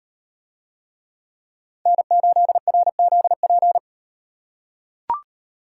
Morse code, heard as a keyed tone, about 37 wpm, 700 Hz. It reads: N8RZP